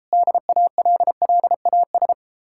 Morse code, heard as a keyed tone, about 33 wpm, 700 Hz. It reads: DALLAS